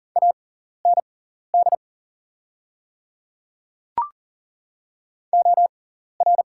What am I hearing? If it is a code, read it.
Morse code, heard as a keyed tone, about 40 wpm, spaced as for 14 wpm, 700 Hz. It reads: AND OR
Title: 6x Character Spacing - Top 300 Words - 40wpm